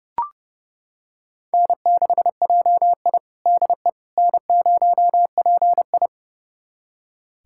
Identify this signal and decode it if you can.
Morse code, heard as a keyed tone, about 30 wpm, 700 Hz. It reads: N6JI DE N0PI
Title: USA Callsign DE Callsign - 4 Characters - 30wpm